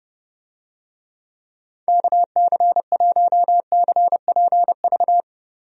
Morse code, heard as a keyed tone, about 30 wpm, 700 Hz. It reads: KC1CPV
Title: USA Call Signs - 6 Characters - Repeated 3x - 30wpm